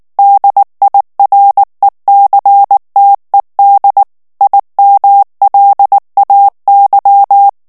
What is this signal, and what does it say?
Morse code, heard as a keyed tone, about 19 wpm, 800 Hz. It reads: DIRECTED IMLAY
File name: ANGLAIS texte clair 3 vitesse 18 mots minute